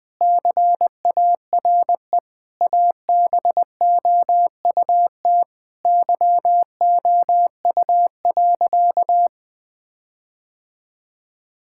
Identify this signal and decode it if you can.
Morse code, heard as a keyed tone, about 20 wpm, 700 Hz. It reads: CARE ABOUT YOU.
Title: Sentences with 7-Word Phrases - 20wpm